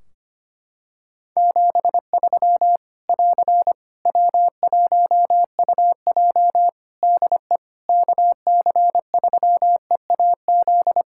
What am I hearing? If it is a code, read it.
Morse code, heard as a keyed tone, about 25 wpm, 700 Hz. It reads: 73 <AR> W1UJ DE KC3EAZ
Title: Practice QSO - Exchange 4 - Ending - 25wpm